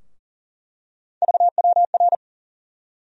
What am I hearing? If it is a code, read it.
Morse code, heard as a keyed tone, about 40 wpm, 700 Hz. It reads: VWR
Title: USA Call Sign Postfixes - 3 Letters 40wpm